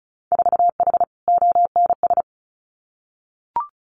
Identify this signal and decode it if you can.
Morse code, heard as a keyed tone, about 34 wpm, 700 Hz. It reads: 4H ONS
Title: Winter Field Day Exchanges Encore 35wpm